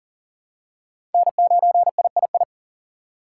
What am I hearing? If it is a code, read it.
Morse code, heard as a keyed tone, about 40 wpm, 700 Hz. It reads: N9III
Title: USA Call Signs - 2x Character Spacing - 40wpm